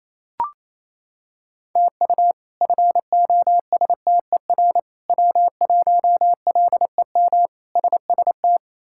Morse code, heard as a keyed tone, about 27 wpm, 700 Hz. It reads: TU FOSTER W1LEM SST
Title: K1USN SST QSO Practice - 28wpm